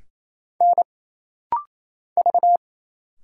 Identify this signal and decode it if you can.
Morse code, heard as a keyed tone, about 28 wpm, 700 Hz. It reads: N V
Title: Single Letters - Rapid-Fire 28wpm